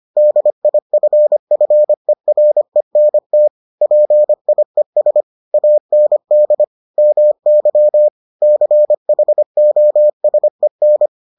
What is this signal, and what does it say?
Morse code, heard as a keyed tone, about 25 wpm, 600 Hz. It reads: DIFFERENT PIES AND MY CHOSEN